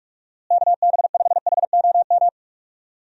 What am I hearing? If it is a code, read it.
Morse code, heard as a keyed tone, about 44 wpm, 700 Hz. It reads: KB5HOM
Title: USA Call Signs - 6 Characters - Repeated 3x - 45wpm